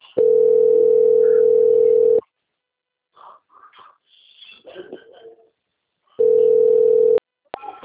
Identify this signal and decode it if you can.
Ringback tone (North America), 440 + 480 Hz.